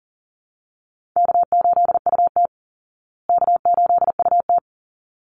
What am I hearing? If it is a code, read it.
Morse code, heard as a keyed tone, about 40 wpm, 700 Hz. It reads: K8UT K8UT
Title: Call Signs Repeated 3x - 40wpm